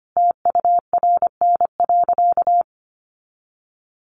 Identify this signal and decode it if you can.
Morse code, heard as a keyed tone, about 25 wpm, 700 Hz. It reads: TURN.
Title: Sentences from Top 600 Words 25wpm